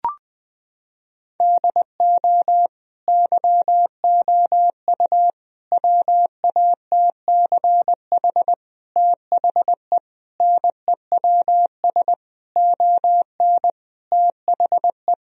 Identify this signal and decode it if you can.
Morse code, heard as a keyed tone, about 20 wpm, 700 Hz. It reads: DO YOU WATCH THE NEWS ON THE
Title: Sentences from Top 1000 Words - Review Part 4 - 20wpm